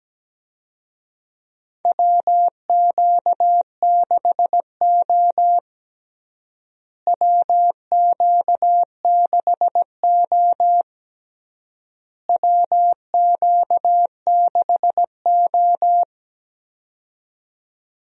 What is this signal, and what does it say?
Morse code, heard as a keyed tone, about 17 wpm, 700 Hz. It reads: WQ6O WQ6O WQ6O
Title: USA Call Signs - 4 Characters - Repeated 3x - 17wpm